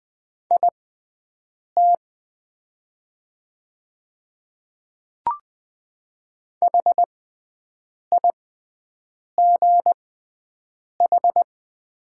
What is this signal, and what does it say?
Morse code, heard as a keyed tone, about 20 wpm, spaced as for 7 wpm, 700 Hz. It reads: IT HIGH
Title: 6x Character Spacing - Top 500 Words - 20wpm